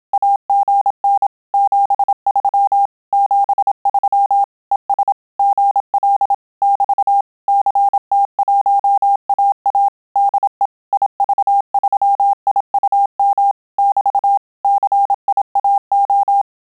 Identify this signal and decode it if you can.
Morse code, heard as a keyed tone, about 27 wpm, 800 Hz. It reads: AGN 73 73 ES GL <BT> CT1AA DE IV3IUM <BT> CIAO
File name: Telegraphy-group / QSO CW 25 PPM